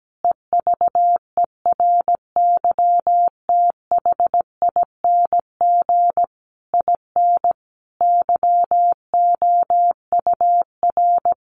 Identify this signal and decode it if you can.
Morse code, heard as a keyed tone, about 17 wpm, 700 Hz. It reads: EVERYTHING IN YOUR
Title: Sentences from Top 700 Words 17wpm